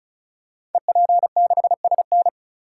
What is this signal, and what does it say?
Morse code, heard as a keyed tone, about 36 wpm, 690 Hz. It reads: EP6SN
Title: Code Groups - 5 Letter:Numbers - 35wpm